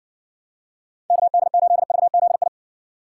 Morse code, heard as a keyed tone, about 55 wpm, 700 Hz. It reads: KD8VZI